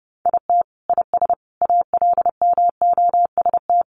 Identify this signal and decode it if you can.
Morse code, heard as a keyed tone, about 30 wpm, 700 Hz. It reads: IT IS ALMOST
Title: Sentences from Top 1200 Words 30wpm